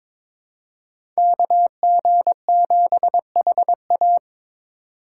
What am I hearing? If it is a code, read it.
Morse code, heard as a keyed tone, about 22 wpm, 700 Hz. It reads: KG7HA
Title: USA Call Signs - 3x Character Spacing - 22wpm